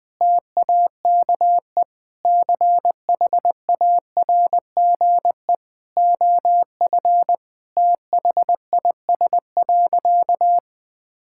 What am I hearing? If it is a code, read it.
Morse code, heard as a keyed tone, about 21 wpm, 700 Hz. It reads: TAKE CHARGE OF THIS.
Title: Sentences from Top 1000 Words - Review Part 6 - 20wpm